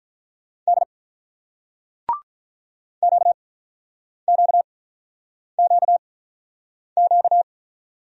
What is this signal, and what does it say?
Morse code, heard as a keyed tone, about 48 wpm, spaced as for 9 wpm, 700 Hz. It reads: D QQQQ